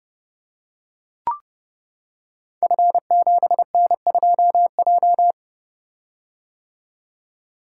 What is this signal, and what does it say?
Morse code, heard as a keyed tone, about 29 wpm, 700 Hz. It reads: F7N2J
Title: Code Groups - 5 Letter:Numbers - 30wpm